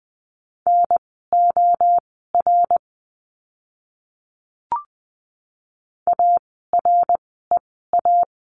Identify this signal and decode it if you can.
Morse code, heard as a keyed tone, about 20 wpm, spaced as for 14 wpm, 700 Hz. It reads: NOR AREA
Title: 054 Lesson 10f - Letter r - Words 20wpm